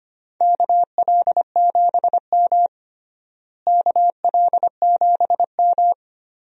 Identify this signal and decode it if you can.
Morse code, heard as a keyed tone, about 25 wpm, 700 Hz. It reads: KL7M KL7M
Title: DXCC-100 Call Signs - 4 Characters - Repeated 3x - 25wpm